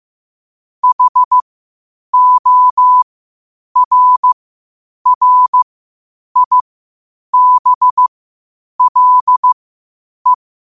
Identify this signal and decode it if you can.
Morse code, heard as a keyed tone, about 15 wpm, spaced as for 8 wpm, 1000 Hz. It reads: HORRIBLE